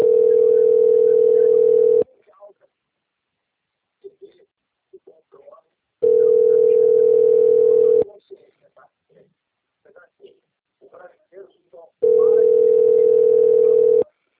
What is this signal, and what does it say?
Ringback tone (North America), 440 + 480 Hz, with speech.
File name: out- - - - - .32330